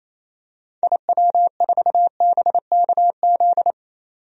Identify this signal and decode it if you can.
Morse code, heard as a keyed tone, about 28 wpm, 700 Hz. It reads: IW4BKZ
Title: DXCC-100 Call Signs - 6 Characters - 28wpm